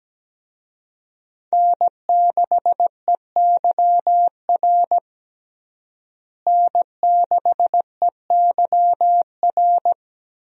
Morse code, heard as a keyed tone, about 17 wpm, 700 Hz. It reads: N6EYR N6EYR
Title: USA Call Signs - 5 Characters - Repeated 3x - 17wpm